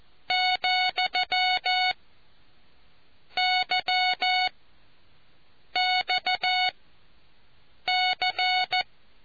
Morse code, heard as a keyed tone, about 14 wpm, spaced as for 6 wpm, 740 Hz. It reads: , Y XC